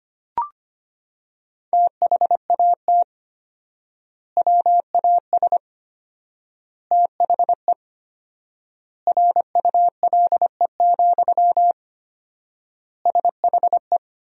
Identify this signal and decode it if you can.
Morse code, heard as a keyed tone, about 25 wpm, 700 Hz. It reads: THAT WAS THE RULE, SHE
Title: 4x Word-Spacing - Sentences from Top 400 Words - 25wpm